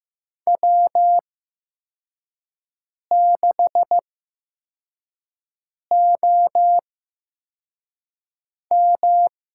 Morse code, heard as a keyed tone, about 15 wpm, spaced as for 4 wpm, 700 Hz. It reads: W6OM
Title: USA Call Signs - 4 Characters - 8x Character Spacing - 15wpm